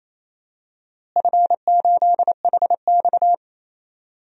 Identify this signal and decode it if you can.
Morse code, heard as a keyed tone, about 28 wpm, 700 Hz. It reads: F8HX
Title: DXCC-100 Call Signs - 4 Characters - Repeated 2x - 28wpm